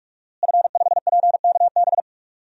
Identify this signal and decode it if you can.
Morse code, heard as a keyed tone, about 45 wpm, 700 Hz. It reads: F5PKB
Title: DXCC-100 Call Signs - 5 Characters - Repeated 3x - 45wpm